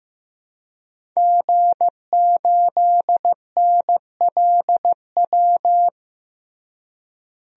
Morse code, heard as a keyed tone, about 16 wpm, 700 Hz. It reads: G8NLW
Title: DXCC-100 Call Signs - 5 Characters - Repeated 2x - 15wpm